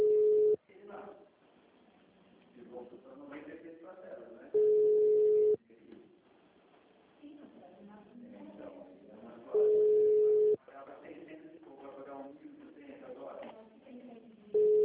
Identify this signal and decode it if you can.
Ringback tone (Europe, 425 Hz), with speech.